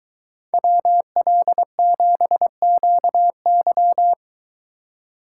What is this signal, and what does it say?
Morse code, heard as a keyed tone, about 22 wpm, 700 Hz. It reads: WL7QY